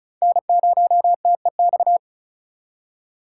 Morse code, heard as a keyed tone, about 48 wpm, spaced as for 42 wpm, 700 Hz. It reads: N0TEX
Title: Speed Racing - USA 1x3 Call Signs 35wpm